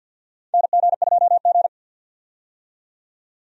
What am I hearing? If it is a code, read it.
Morse code, heard as a keyed tone, about 52 wpm, 700 Hz. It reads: NG2G
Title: USA Call Signs - 4 Characters - 50wpm